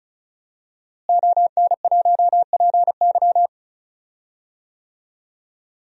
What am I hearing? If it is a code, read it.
Morse code, heard as a keyed tone, about 34 wpm, 700 Hz. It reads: ON1PY
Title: DXCC-100 Call Signs - 5 Characters - Repeated 2x - 35wpm